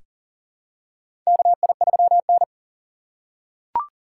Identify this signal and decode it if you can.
Morse code, heard as a keyed tone, about 40 wpm, 700 Hz. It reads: KI3N
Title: USA Call Signs - 4x Character Spacing - 40wpm